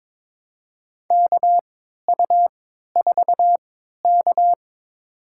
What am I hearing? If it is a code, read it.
Morse code, heard as a keyed tone, about 22 wpm, spaced as for 13 wpm, 700 Hz. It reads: KU4K
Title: USA Call Signs - 4 Characters - 3x Character Spacing - 22wpm